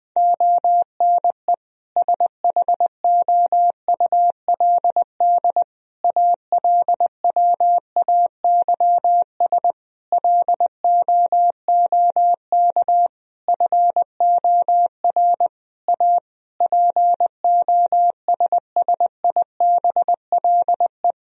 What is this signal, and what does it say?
Morse code, heard as a keyed tone, about 20 wpm, 700 Hz. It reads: ONE SHOULD ALWAYS LOOK FOR A POSSIBLE
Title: Sentences from Top 1000 Words - Review Part 4 - 20wpm